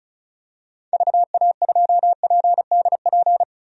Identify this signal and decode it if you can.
Morse code, heard as a keyed tone, about 36 wpm, 700 Hz. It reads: VA2PDP
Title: DXCC-100 Call Signs - 6 Characters - Repeated 2x - 35wpm